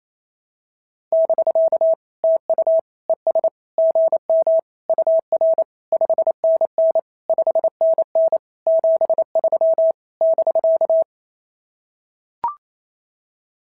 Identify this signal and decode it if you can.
Morse code, heard as a keyed tone, about 25 wpm, 650 Hz. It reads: <BK> TU ES GM UR 5NN 5NN 73 <BK>
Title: SOTA QSO Practice 28wpm